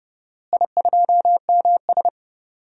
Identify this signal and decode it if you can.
Morse code, heard as a keyed tone, about 30 wpm, 700 Hz. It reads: I2MS
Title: DXCC-100 Call Signs - 4 Characters - Repeated 2x - 30wpm